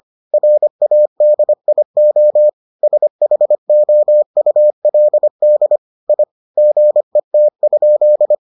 Morse code, heard as a keyed tone, about 25 wpm, 600 Hz. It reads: RADIO SHOULD I GET?